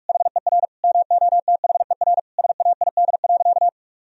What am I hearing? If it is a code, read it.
Morse code, heard as a keyed tone, about 46 wpm, 700 Hz. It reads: HER MOTHER SAID.
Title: Sentences from Top 1000 Words - Review Part 4 - 45wpm